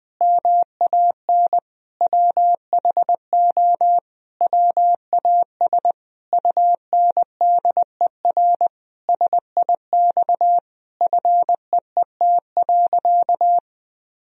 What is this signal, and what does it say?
Morse code, heard as a keyed tone, about 20 wpm, 700 Hz. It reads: MAN WHO WAS UNDER SIX FEET.
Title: Sentences from Top 1000 Words - Review Part 2 - 20wpm